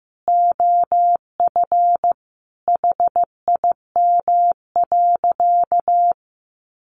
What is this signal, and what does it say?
Morse code, heard as a keyed tone, about 15 wpm, 700 Hz. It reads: OF HIM.
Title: Sentences from Top 300 Words 15wpm